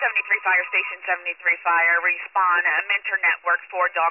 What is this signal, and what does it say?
Emergency Alert System (SAME) tones, heard as harsh buzzing bursts, with speech. It is heard from 0.2 to 3.6 s.